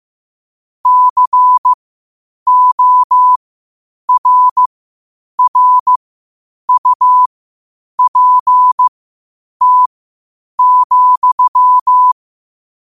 Morse code, heard as a keyed tone, about 15 wpm, spaced as for 8 wpm, 1000 Hz. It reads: CORRUPT,
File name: qotdfiles / 8wpm